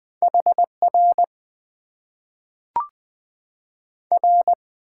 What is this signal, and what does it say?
Morse code, heard as a keyed tone, about 20 wpm, 700 Hz. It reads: HR R
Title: Top 90 QSO Elements - 6x Character Spacing - 20wpm